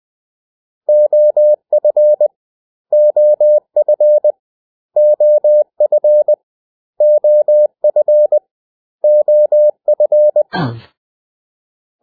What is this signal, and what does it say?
Morse code, heard as a keyed tone, about 20 wpm, 600 Hz. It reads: OF OF OF OF OF
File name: cw / 100 Common English Words